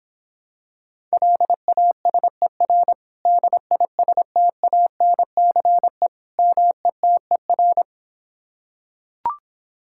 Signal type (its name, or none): Morse code